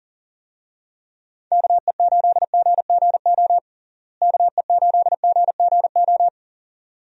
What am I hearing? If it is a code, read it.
Morse code, heard as a keyed tone, about 40 wpm, 700 Hz. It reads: KE8GGO KE8GGO
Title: USA Call Signs - 6 Characters - Repeated 2x - 40wpm